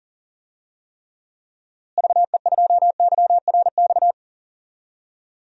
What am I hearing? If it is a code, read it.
Morse code, heard as a keyed tone, about 37 wpm, 700 Hz. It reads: VE2YRX
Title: DXCC-100 Call Signs - 6 Characters - Repeated 3x - 40wpm